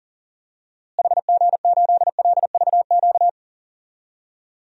Morse code, heard as a keyed tone, about 40 wpm, 700 Hz. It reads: HG8LVQ
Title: DXCC-100 Call Signs - 6 Characters - Repeated 3x - 40wpm